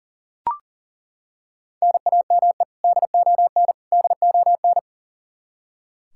Morse code, heard as a keyed tone, about 40 wpm, 700 Hz. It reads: NAME DON DON
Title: Practice QSO - Exchange 1 - Beginning - 40wpm